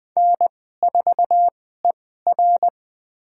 Morse code, heard as a keyed tone, about 20 wpm, spaced as for 14 wpm, 700 Hz. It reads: N4ER
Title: USA Call Signs - 4 Characters - 2x Character Spacing - 20wpm